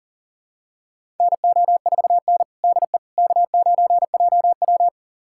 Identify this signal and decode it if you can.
Morse code, heard as a keyed tone, about 40 wpm, 700 Hz. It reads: NO4N DE K9JW
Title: USA Callsign DE Callsign - 4 Characters - 40wpm